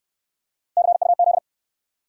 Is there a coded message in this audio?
Yes: Morse code from 0.8 to 1.4 s, heard as a keyed tone, about 63 wpm, 700 Hz. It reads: CUZ